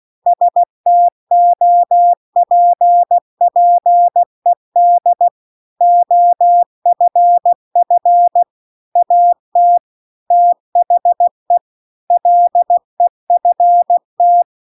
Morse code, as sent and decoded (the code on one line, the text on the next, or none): ... - --- .--. .--. -.. / --- ..-. ..-. / .- - / - .... / .-.. ..-. -
STOPPED OFF AT THE LEFT